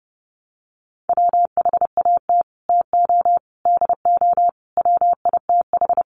Morse code, heard as a keyed tone, about 30 wpm, 700 Hz. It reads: WHAT TO DO WITH
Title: Sets of 5 Words Encore2 30wpm